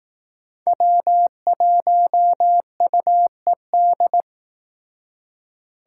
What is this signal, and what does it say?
Morse code, heard as a keyed tone, about 18 wpm, 700 Hz. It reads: W1UED
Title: Speed Racing - USA 1x3 Call Signs 15wpm